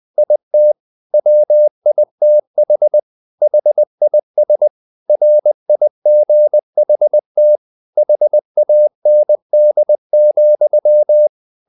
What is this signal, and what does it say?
Morse code, heard as a keyed tone, about 20 wpm, 600 Hz. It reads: IT WITH HIS RIGHT HAND,
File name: the old man and the sea part 2 20wpm